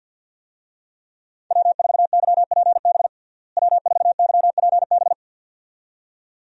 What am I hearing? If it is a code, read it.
Morse code, heard as a keyed tone, about 52 wpm, 700 Hz. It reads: W4YPB W4YPB
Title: USA Call Signs - 5 Characters - Repeated 2x - 50wpm